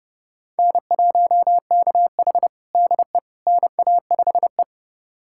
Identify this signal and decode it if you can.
Morse code, heard as a keyed tone, about 30 wpm, 700 Hz. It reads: N1KH DE NA5E
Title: USA Callsign DE Callsign - 4 Characters - 30wpm